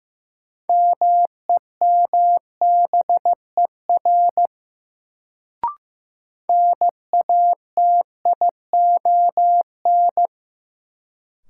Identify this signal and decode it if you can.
Morse code, heard as a keyed tone, about 15 wpm, 700 Hz. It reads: MEMBER NATION
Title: Sets of 3 Words - New Word Drill - 15wpm